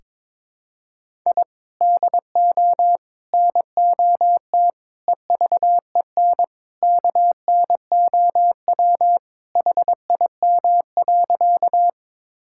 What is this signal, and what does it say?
Morse code, heard as a keyed tone, about 22 wpm, 700 Hz. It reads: I DO NOT EVEN KNOW HIM.
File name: practice / 22wpm / Sentences with 4-Word Phrases - 22wpm